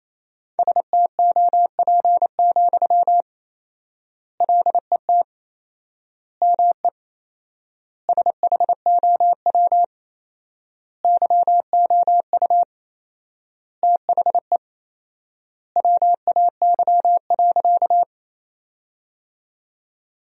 Morse code, heard as a keyed tone, about 28 wpm, 700 Hz. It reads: STOP, LET ME SHOW YOU THE WAY.
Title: 4x Word-Spacing - Sentences from Top 300 Words - 28wpm